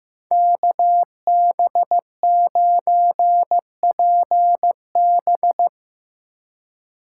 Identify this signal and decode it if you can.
Morse code, heard as a keyed tone, about 16 wpm, 700 Hz. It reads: KB9PB
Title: Speed Racing - USA 2x2 Call Signs 15wpm